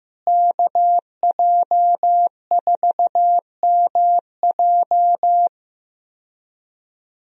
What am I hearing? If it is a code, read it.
Morse code, heard as a keyed tone, about 15 wpm, 700 Hz. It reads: KJ4MJ
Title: USA Call Signs - 5 Characters - Repeated 2x - 15wpm